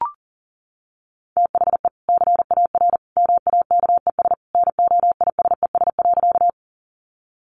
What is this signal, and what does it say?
Morse code, heard as a keyed tone, about 40 wpm, 700 Hz. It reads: THE CAR MAKES NOISES.
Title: Easy Phrases 40wpm